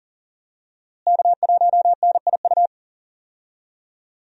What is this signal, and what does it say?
Morse code, heard as a keyed tone, about 40 wpm, 700 Hz. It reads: K1NIU
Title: USA Call Signs - 5 Characters - Repeated 2x - 40wpm